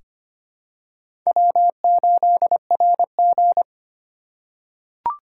Morse code, heard as a keyed tone, about 25 wpm, 700 Hz. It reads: W8RG
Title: USA Call Signs - 4 Characters - 4x Character Spacing - 25wpm